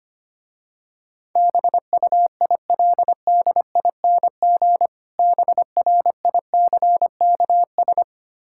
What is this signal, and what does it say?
Morse code, heard as a keyed tone, about 25 wpm, 700 Hz. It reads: BUILDING BRICKS
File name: practice / 25wpm / Difficult Words with Hints - 25wpm